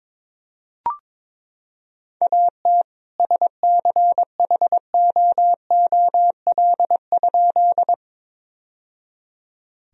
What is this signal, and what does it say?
Morse code, heard as a keyed tone, about 22 wpm, 700 Hz. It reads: AT SCHOOL?
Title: Sentences from Top 1000 Words - Review Part 1 - 22wpm